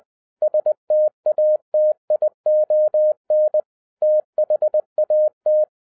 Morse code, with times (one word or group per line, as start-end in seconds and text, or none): 0.4-3.6 STATION
4.0-5.6 THAT